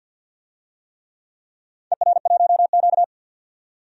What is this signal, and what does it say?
Morse code, heard as a keyed tone, about 47 wpm, 700 Hz. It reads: ER1Q
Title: DXCC-100 Call Signs - 4 Characters - 50wpm